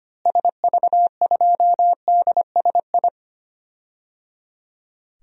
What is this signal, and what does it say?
Morse code, heard as a keyed tone, about 25 wpm, 700 Hz. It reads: SV2DSI
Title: DXCC-100 Call Signs - 6 Characters - Repeated 3x - 25wpm